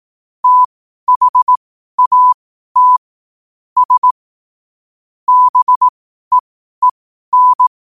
Morse code, heard as a keyed tone, about 18 wpm, spaced as for 12 wpm, 1000 Hz. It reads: THAT S BEEN
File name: qotdfiles / 12wpm